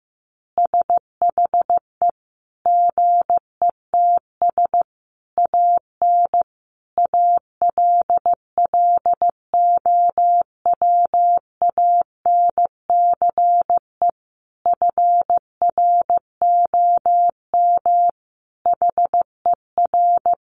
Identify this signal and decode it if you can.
Morse code, heard as a keyed tone, about 15 wpm, 700 Hz. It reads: SHE GETS AN ALLOWANCE FROM HER
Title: Easy Phrases 15wpm